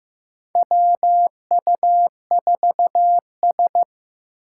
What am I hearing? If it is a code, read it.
Morse code, heard as a keyed tone, about 15 wpm, 700 Hz. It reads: WU4S
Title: USA Call Signs - 4 Characters - 2x Character Spacing - 15wpm